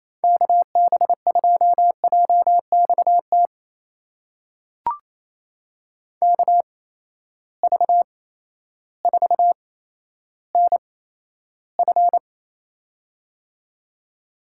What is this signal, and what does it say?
Morse code, heard as a keyed tone, about 28 wpm, spaced as for 13 wpm, 700 Hz. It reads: KB2JXT KV4NF
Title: USA Call Signs - 8x Character Spacing - 28wpm